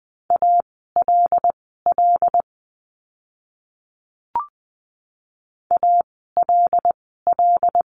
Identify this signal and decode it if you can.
Morse code, heard as a keyed tone, about 20 wpm, spaced as for 15 wpm, 700 Hz. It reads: ALL ALL
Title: 237 Lesson 38f - Letter q - Words 20wpm